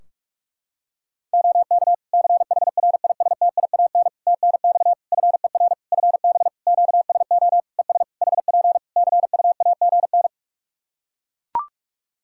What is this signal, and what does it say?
Morse code, heard as a keyed tone, about 42 wpm, 700 Hz. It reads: OK CHRISTIAN TNX FER FB QSO ES HP CUAGN
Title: Practice QSO - Exchange 4 - Ending - 45wpm